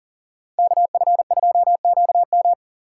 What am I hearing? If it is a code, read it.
Morse code, heard as a keyed tone, about 40 wpm, 700 Hz. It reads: KF2QM
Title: USA Call Signs - 5 Characters - 3x Character Spacing - 40wpm